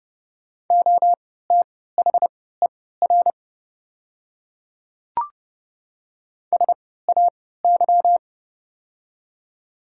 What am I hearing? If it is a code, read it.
Morse code, heard as a keyed tone, about 30 wpm, spaced as for 17 wpm, 690 Hz. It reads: OTHER SAY
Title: 3x Character Spacing - Top 200 Words - 30wpm